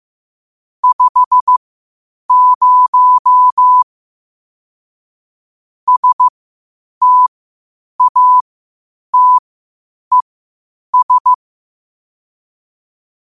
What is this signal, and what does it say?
Morse code, heard as a keyed tone, about 15 wpm, spaced as for 8 wpm, 1000 Hz. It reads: 50 STATES
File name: qotdfiles / 8wpm